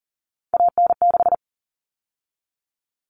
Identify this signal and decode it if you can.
Morse code, heard as a keyed tone, about 40 wpm, 700 Hz. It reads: AN6
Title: All Cut Numbers - 3-Digit 40wpm